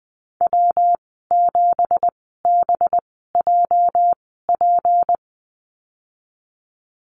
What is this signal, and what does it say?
Morse code, heard as a keyed tone, about 20 wpm, spaced as for 14 wpm, 700 Hz. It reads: W7BJP
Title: 213 Lesson 34g - Number 8 - Call Signs 20wpm